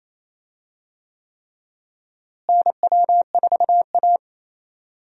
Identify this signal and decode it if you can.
Morse code, heard as a keyed tone, about 27 wpm, 700 Hz. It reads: NW4A
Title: USA Call Signs - 4 Characters - Repeated 3x - 28wpm